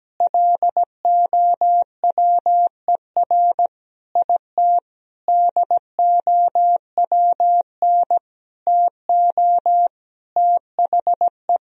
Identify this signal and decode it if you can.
Morse code, heard as a keyed tone, about 17 wpm, 700 Hz. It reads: LOWER IT DOWN TO THE
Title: Sentences from Top 1000 Words - Review Part 4 - 17wpm